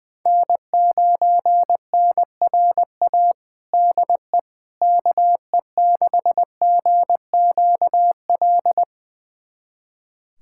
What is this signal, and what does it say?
Morse code, heard as a keyed tone, about 21 wpm, 700 Hz. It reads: N9NRA DE KE6GQL
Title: USA Callsign DE Callsign 20wpm